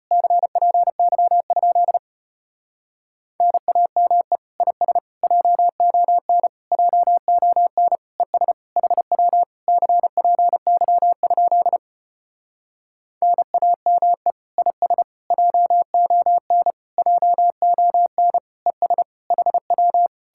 Morse code, heard as a keyed tone, about 33 wpm, 700 Hz. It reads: CPY? NAME IS JON JON ES HW CPY? NAME IS JON JON ES HW